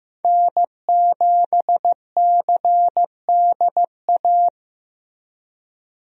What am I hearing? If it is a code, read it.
Morse code, heard as a keyed tone, about 15 wpm, 700 Hz. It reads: N7CDA
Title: USA Call Signs - 5 Characters - Repeated 2x - 15wpm